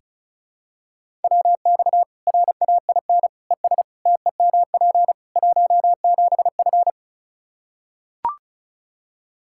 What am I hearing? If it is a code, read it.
Morse code, heard as a keyed tone, about 36 wpm, 700 Hz. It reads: WX RAIN ES TEMP 17F
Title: Practice QSO - Exchange 2 - Info - 35wpm